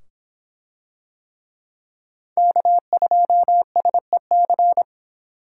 Morse code, heard as a keyed tone, about 25 wpm, 700 Hz. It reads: K2SEC